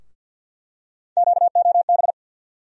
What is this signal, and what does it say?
Morse code, heard as a keyed tone, about 49 wpm, 700 Hz. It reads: QOB
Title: USA Call Sign Postfixes - 3 Letters 50wpm